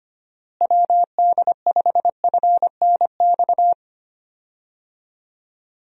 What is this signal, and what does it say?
Morse code, heard as a keyed tone, about 25 wpm, 700 Hz. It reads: WD5FNX
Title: USA Call Signs - 6 Characters - Repeated 2x - 25wpm